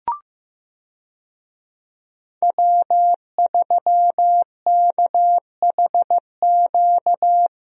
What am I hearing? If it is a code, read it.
Morse code, heard as a keyed tone, about 15 wpm, 700 Hz. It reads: W3KHQ
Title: USA Call Signs - 5 Characters - Repeated 2x - 15wpm